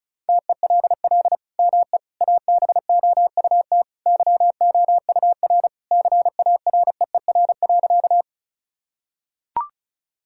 Morse code, heard as a keyed tone, about 36 wpm, 700 Hz. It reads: TELL ME ABOUT YOUR CAREER.
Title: Sentences from Top 1000 Words - Review Part 6 - 35wpm